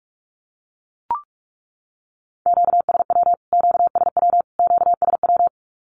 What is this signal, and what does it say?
Morse code, heard as a keyed tone, about 44 wpm, 700 Hz. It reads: QSW QSW QSW
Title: Q-Codes Repeated 3x 45wpm